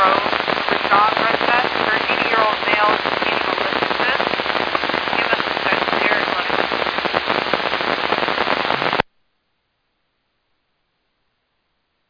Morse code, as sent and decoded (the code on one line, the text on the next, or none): none